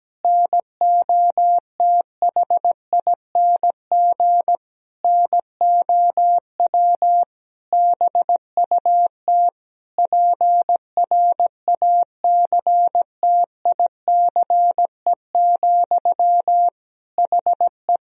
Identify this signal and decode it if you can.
Morse code, heard as a keyed tone, about 17 wpm, 700 Hz. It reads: NOTHING NOW BUT PRACTICE, HE
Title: Sentences from Top 1000 Words - Review Part 4 - 17wpm